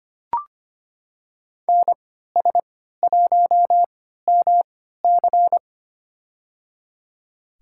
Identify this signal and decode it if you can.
Morse code, heard as a keyed tone, about 25 wpm, spaced as for 14 wpm, 700 Hz. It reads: NS1MC
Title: USA Call Signs - 5 Characters - 3x Character Spacing - 25wpm